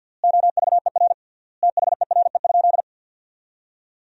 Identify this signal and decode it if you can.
Morse code, heard as a keyed tone, about 50 wpm, 700 Hz. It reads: OVER THERE?
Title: 3x Word-Spacing - Sentences from Top 200 Words - 50wpm